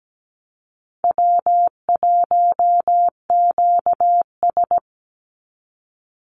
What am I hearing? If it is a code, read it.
Morse code, heard as a keyed tone, about 17 wpm, 700 Hz. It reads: W1QS
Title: Call Signs Repeated 3x - 17wpm